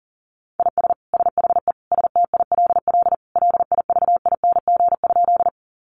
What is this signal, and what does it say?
Morse code, heard as a keyed tone, about 37 wpm, 700 Hz. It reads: IS SHE STILL LIVING?
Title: Sentences from Top 1100 Words 40wpm